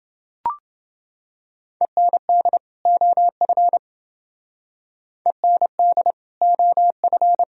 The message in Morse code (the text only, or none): END OF END OF